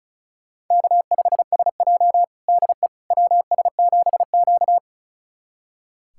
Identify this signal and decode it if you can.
Morse code, heard as a keyed tone, about 36 wpm, 700 Hz. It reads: K5SJ DE WS7Q
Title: USA Callsign DE Callsign - 4 Characters - 35wpm